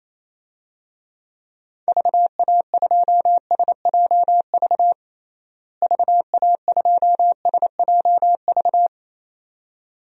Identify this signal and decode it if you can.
Morse code, heard as a keyed tone, about 28 wpm, 700 Hz. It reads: VA2SJV VA2SJV
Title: DXCC-100 Call Signs - 6 Characters - Repeated 2x - 28wpm